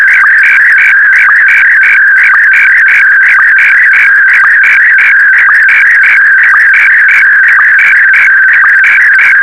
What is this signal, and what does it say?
SSTV (slow-scan television), Scottie DX.